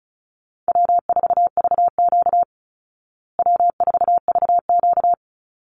Morse code, heard as a keyed tone, about 34 wpm, 700 Hz. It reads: W4VQ W4VQ
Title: Easy Call Signs Repeated 3x - 35wpm